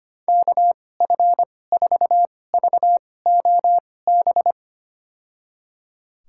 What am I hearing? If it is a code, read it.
Morse code, heard as a keyed tone, about 25 wpm, spaced as for 18 wpm, 690 Hz. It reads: KF4VOB
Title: USA Call Signs - 6 Characters - 2x Character Spacing - 25wpm